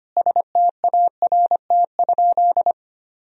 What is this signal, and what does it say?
Morse code, heard as a keyed tone, about 25 wpm, 700 Hz. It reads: START?
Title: Sentences from Top 1000 Words - Review Part 1 - 25wpm